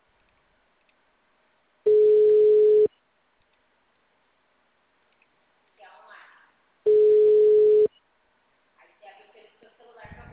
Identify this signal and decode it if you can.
Ringback tone (Europe, 425 Hz), with speech.